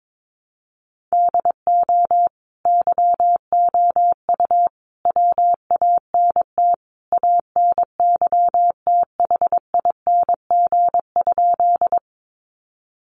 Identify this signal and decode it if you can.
Morse code, heard as a keyed tone, about 22 wpm, 700 Hz. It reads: DO YOU WANT ANYTHING?
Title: Sentences from Top 1000 Words 22wpm